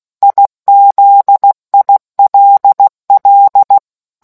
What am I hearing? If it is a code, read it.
Morse code, heard as a keyed tone, about 16 wpm, 800 Hz. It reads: IZILL